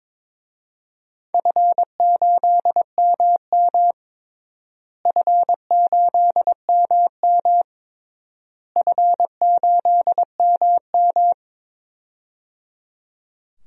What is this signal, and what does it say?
Morse code, heard as a keyed tone, about 22 wpm, 700 Hz. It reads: F8MM F8MM F8MM
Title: DXCC-100 Call Signs - 4 Characters - Repeated 3x - 22wpm